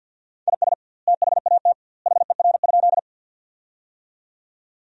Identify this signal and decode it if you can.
Morse code, heard as a keyed tone, about 52 wpm, 700 Hz. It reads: IS THAT HER?
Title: 2x Word-Spacing - Sentences from Top 100 Words - 50wpm